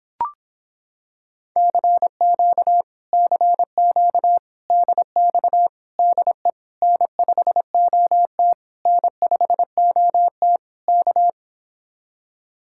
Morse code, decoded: CQ CQ DX DE N5OT N5OT K

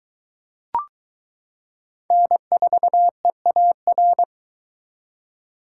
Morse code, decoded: N4EAR

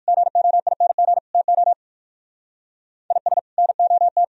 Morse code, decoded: GOING TO IS NOT

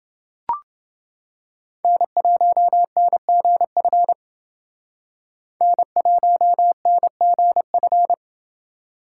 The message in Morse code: N1NGF N1NGF

